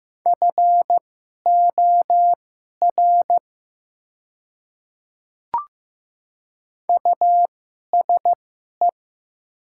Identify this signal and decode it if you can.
Morse code, heard as a keyed tone, about 15 wpm, spaced as for 11 wpm, 700 Hz. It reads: FOR USE
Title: 2x Character Spacing - Top 100 Words - 15wpm